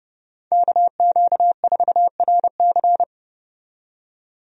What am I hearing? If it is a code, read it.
Morse code, heard as a keyed tone, about 30 wpm, 700 Hz. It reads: KQ4RC
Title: USA Call Signs - 5 Characters - 30wpm